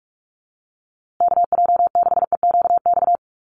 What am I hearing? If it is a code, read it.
Morse code, heard as a keyed tone, about 44 wpm, 700 Hz. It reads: KJ6EQX
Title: USA Call Signs Repeated 2x 45wpm